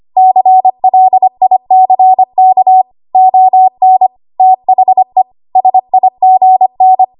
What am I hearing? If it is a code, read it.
Morse code, heard as a keyed tone, about 25 wpm, 750 Hz. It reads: CLICK ON THE SIGN
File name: Archive / 25 WPM / 25WPM